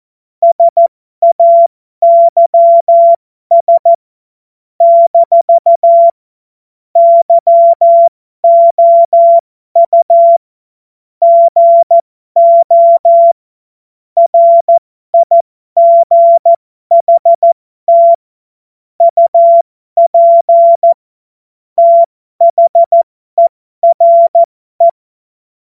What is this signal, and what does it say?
Morse code, heard as a keyed tone, about 14 wpm, spaced as for 12 wpm, 680 Hz. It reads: SAYS - YOU GO RIGHT UP THERE
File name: Morsekurs / Witze